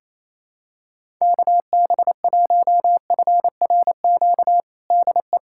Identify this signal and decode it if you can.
Morse code, heard as a keyed tone, about 28 wpm, 700 Hz. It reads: KB1FRQ DE
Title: USA Callsign DE Callsign - 6 Characters - 28wpm